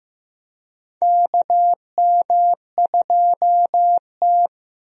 Morse code, heard as a keyed tone, about 15 wpm, 700 Hz. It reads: KM2T
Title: USA Call Signs - 4 Characters - Repeated 2x - 15wpm